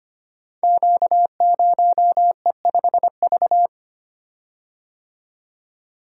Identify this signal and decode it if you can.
Morse code, heard as a keyed tone, about 25 wpm, 700 Hz. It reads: Q0E5V
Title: Code Groups - 5 Letter:Numbers - 25wpm